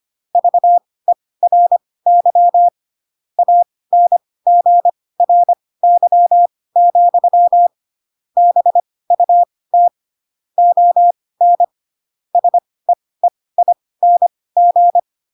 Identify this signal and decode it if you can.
Morse code, heard as a keyed tone, about 25 wpm, spaced as for 18 wpm, 700 Hz. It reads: VERY ANGRY, BUT ON SEEING
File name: Pinocchio / Pinocchio ch7 12 25 ik1hge